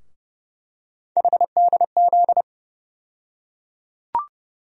Morse code, heard as a keyed tone, about 30 wpm, 700 Hz. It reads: HDZ
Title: USA Call Sign Postfixes - 3 Letters 30wpm